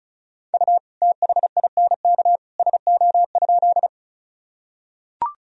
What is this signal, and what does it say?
Morse code, heard as a keyed tone, about 34 wpm, 700 Hz. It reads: U THINK SO?
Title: Sentences from Top 200 Words and Abbreviations 35wpm